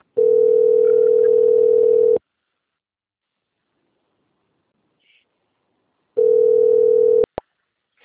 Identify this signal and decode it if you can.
Ringback tone (North America), 440 + 480 Hz.